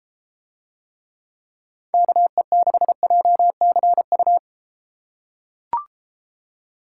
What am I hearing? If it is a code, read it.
Morse code, heard as a keyed tone, about 32 wpm, 700 Hz. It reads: KE6JCU